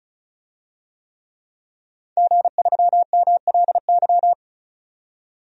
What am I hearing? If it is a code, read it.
Morse code, heard as a keyed tone, about 36 wpm, 700 Hz. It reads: G3MLY
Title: DXCC-100 Call Signs - 5 Characters - 35wpm